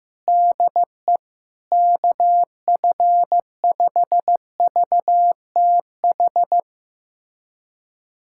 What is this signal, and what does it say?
Morse code, heard as a keyed tone, about 15 wpm, 700 Hz. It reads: DE KF5VTH
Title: USA Callsign DE Callsign - 6 Characters - 15wpm